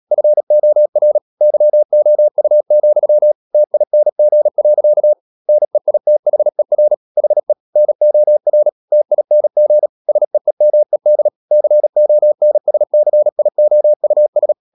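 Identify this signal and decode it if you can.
Morse code, heard as a keyed tone, about 37 wpm, 600 Hz. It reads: FOR YOU, TING. NEITHER HE NOR TING SEEMED CONSCIOUS
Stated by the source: Morse code practice at 35–39 wpm sending English text of Ken Follett's novel Never, from CW med Gnister.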